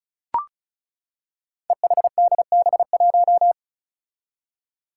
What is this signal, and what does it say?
Morse code, heard as a keyed tone, about 36 wpm, 700 Hz. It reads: EHDB1
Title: Five Challenging Letter-Number Combinations 35wpm